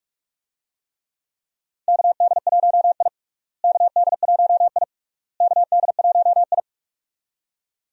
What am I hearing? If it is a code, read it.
Morse code, heard as a keyed tone, about 45 wpm, 700 Hz. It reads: KD1I KD1I KD1I
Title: USA Call Signs - 4 Characters - Repeated 3x - 45wpm